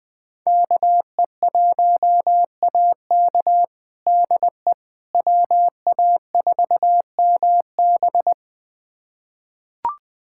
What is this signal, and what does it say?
Morse code, heard as a keyed tone, about 21 wpm, 700 Hz. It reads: KE1AK DE WA4MB
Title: USA Callsign DE Callsign - 5 Characters - 20wpm